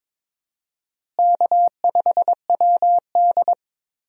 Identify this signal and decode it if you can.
Morse code, heard as a keyed tone, about 22 wpm, 700 Hz. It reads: K5WD
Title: USA Call Signs - 4 Characters - Repeated 3x - 22wpm